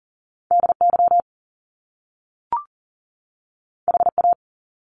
Morse code, heard as a keyed tone, about 42 wpm, 700 Hz. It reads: DY HA
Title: Top Two Letter Combinations in Words 40wpm